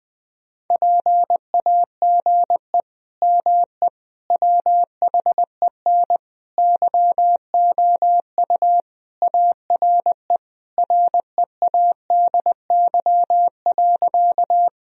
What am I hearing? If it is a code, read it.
Morse code, heard as a keyed tone, about 20 wpm, 700 Hz. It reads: PAGE ME WHEN YOU ARE READY.
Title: Sentences from Top 1000 Words - Review Part 1 - 20wpm